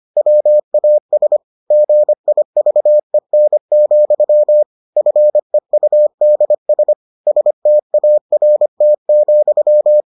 Morse code, heard as a keyed tone, about 25 wpm, 600 Hz. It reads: WAS GIVEN, FEUDS START,